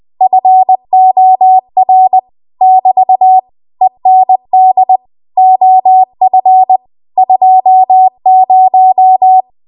Morse code, heard as a keyed tone, about 20 wpm, 750 Hz. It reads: FOR <BT> END OF 20
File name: Archive / 20 WPM / 20WPM